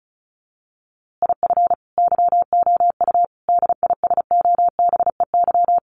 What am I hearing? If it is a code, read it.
Morse code, heard as a keyed tone, about 34 wpm, 700 Hz. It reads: IF YOU DISOBEY
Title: A Sound Of Thunder 35wpm